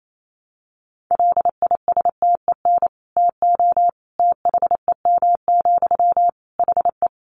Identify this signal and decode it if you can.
Morse code, heard as a keyed tone, about 28 wpm, 700 Hz. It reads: LISTEN TO THEM, HE
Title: Sentences from Top 400 Words 28wpm